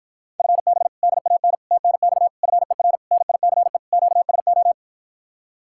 Morse code, heard as a keyed tone, about 54 wpm, 700 Hz. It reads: FB DAN TNX FER NICE QSO